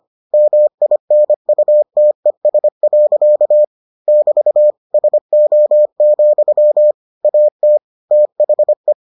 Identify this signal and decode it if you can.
Morse code, heard as a keyed tone, about 25 wpm, 600 Hz. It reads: MINUTES. <BT> SO, AT THE